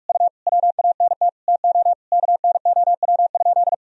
Morse code, heard as a keyed tone, about 42 wpm, 700 Hz. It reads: U WANT TO KNOW?
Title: Sentences from Top 200 Words and Abbreviations 45wpm